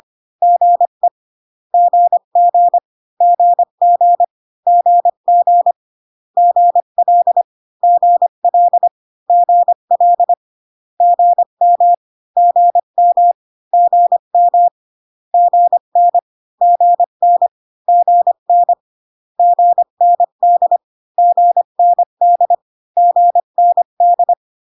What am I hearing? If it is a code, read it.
Morse code, heard as a keyed tone, about 25 wpm, 700 Hz. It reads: GE GG GG GG GL GL GL GM GM GM GN GN GN GND GND GND